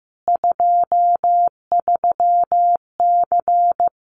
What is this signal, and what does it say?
Morse code, heard as a keyed tone, about 15 wpm, 700 Hz. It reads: 23C
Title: Common QSO Phrases 15wpm